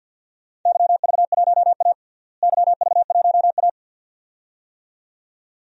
Morse code, heard as a keyed tone, about 47 wpm, 700 Hz. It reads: YV1A YV1A
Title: DXCC-100 Call Signs - 4 Characters - Repeated 2x - 50wpm